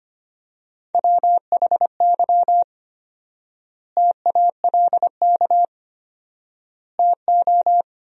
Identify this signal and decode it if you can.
Morse code, heard as a keyed tone, about 25 wpm, 700 Hz. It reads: WHY TALK TO
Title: 4x Word-Spacing - Sentences from Top 400 Words - 25wpm